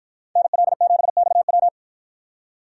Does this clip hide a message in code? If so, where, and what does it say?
Yes: Morse code from 0.4 to 1.7 s, heard as a keyed tone, about 53 wpm, 700 Hz. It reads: NL7XW